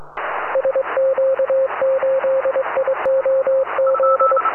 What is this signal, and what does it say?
Morse code, heard as a keyed tone, about 23 wpm, 530 Hz, with radio noise. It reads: SQ8IOZ